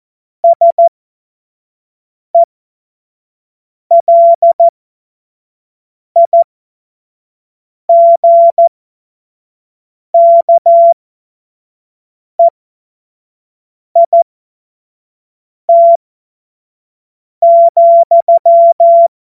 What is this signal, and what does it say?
Morse code, heard as a keyed tone, about 14 wpm, spaced as for 5 wpm, 680 Hz. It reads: SELIGKEIT,
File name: Morsekurs / Witze / Witz 21. Dezember 2021, 50 F70